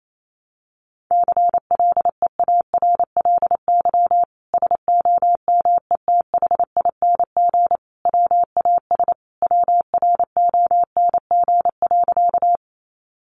Morse code, heard as a keyed tone, about 28 wpm, 700 Hz. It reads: CLEARLY SOMETHING WAS WRONG.
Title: Sentences from Top 1300 Words 28wpm